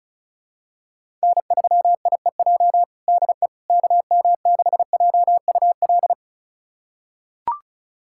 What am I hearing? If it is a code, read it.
Morse code, heard as a keyed tone, about 34 wpm, 700 Hz. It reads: N3IEJ DE KM6JUL
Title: USA Callsign DE Callsign 35wpm